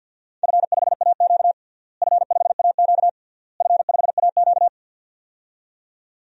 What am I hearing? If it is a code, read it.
Morse code, heard as a keyed tone, about 51 wpm, 700 Hz. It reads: F5AQ F5AQ F5AQ
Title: DXCC-100 Call Signs - 4 Characters - Repeated 3x - 50wpm